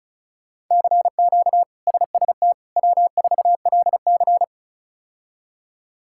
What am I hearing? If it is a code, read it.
Morse code, heard as a keyed tone, about 36 wpm, 700 Hz. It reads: CQ SST W4LC
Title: K1USN SST QSO Practice - 35wpm